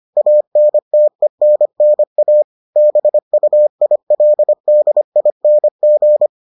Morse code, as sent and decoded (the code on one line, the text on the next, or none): .- -. - . -. -. .- / -... ..- .. .-.. -.. .. -. --.
ANTENNA BUILDING